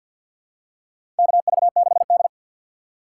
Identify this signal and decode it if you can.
Morse code, heard as a keyed tone, about 53 wpm, 700 Hz. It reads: KV6D
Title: USA Call Signs - 4 Characters - 6x Character Spacing - 50wpm